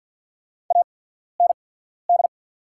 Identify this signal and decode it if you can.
Morse code, heard as a keyed tone, about 48 wpm, spaced as for 14 wpm, 700 Hz. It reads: AND
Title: 8x Character Spacing - Top 200 Words - 50wpm